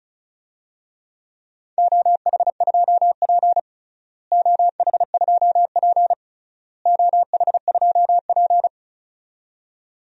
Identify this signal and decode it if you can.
Morse code, heard as a keyed tone, about 36 wpm, 700 Hz. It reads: OH2P OH2P OH2P
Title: DXCC-100 Call Signs - 4 Characters - Repeated 3x - 35wpm